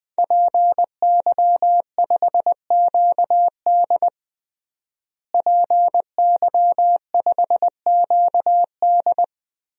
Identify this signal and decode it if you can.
Morse code, heard as a keyed tone, about 20 wpm, 700 Hz. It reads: PY5QD PY5QD
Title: DXCC-100 Call Signs - 5 Characters - Repeated 3x - 20wpm